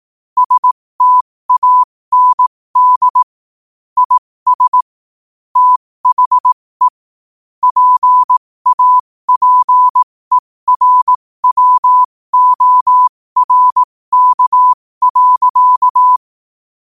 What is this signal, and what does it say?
Morse code, heard as a keyed tone, about 18 wpm, spaced as for 15 wpm, 1000 Hz. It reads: STAND IS THE PAPERWORK.
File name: qotdfiles / 15wpm